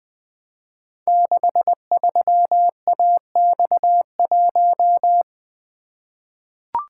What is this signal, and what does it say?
Morse code, heard as a keyed tone, about 20 wpm, 700 Hz. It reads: 63AX1
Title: Code Groups - 5 Letter:Numbers - 20wpm